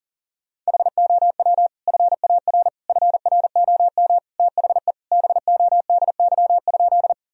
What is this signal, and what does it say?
Morse code, heard as a keyed tone, about 40 wpm, 700 Hz. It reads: HOW FAR FROM THE BODY?
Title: Sentences from Top 1000 Words - Review Part 2 - 40wpm